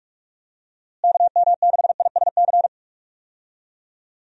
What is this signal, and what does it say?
Morse code, heard as a keyed tone, about 43 wpm, 700 Hz. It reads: KM6ISC
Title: USA Call Signs - 6 Characters - Repeated 3x - 45wpm